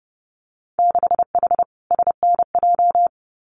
Morse code, heard as a keyed tone, about 30 wpm, 700 Hz. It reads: 6H SNJ
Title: Winter Field Day Exchanges Encore 30wpm